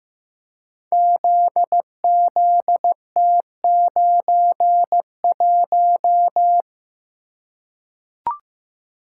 Morse code, heard as a keyed tone, about 15 wpm, 700 Hz. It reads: ZZT91
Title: Code Groups - 5 Letter:Numbers - 15wpm